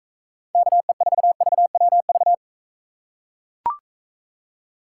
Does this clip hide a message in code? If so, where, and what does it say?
Yes: Morse code from 0.5 to 2.3 s, heard as a keyed tone, about 42 wpm, 700 Hz. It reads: KE4VWV